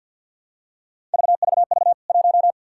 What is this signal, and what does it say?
Morse code, heard as a keyed tone, about 50 wpm, 700 Hz. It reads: VVV 1
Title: K1USN SST QSO Practice 50wpm